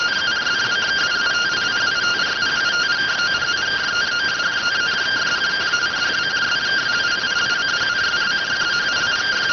Two-tone teleprinter signal, about 170 Hz shift.